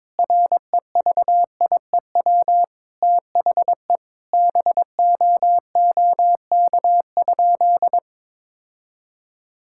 Morse code, heard as a keyed tone, about 22 wpm, 700 Hz. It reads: REVIEW THE BOOK?
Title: Sentences from Top 1000 Words - Review Part 6 - 22wpm